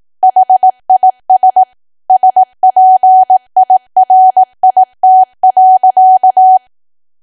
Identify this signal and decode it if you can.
Morse code, heard as a keyed tone, about 18 wpm, 750 Hz. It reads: HIS SPIRIT.
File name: Archive / 18 WPM / 18WPM